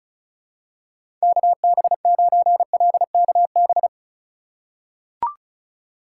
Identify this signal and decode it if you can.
Morse code, heard as a keyed tone, about 36 wpm, 700 Hz. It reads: KB9LKB
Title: USA Call Signs - 6 Characters - 3x Character Spacing - 35wpm